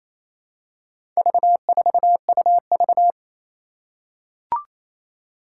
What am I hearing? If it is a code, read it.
Morse code, heard as a keyed tone, about 28 wpm, 700 Hz. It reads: V4UV
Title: Four Challenging Letter-Number Combinations 28wpm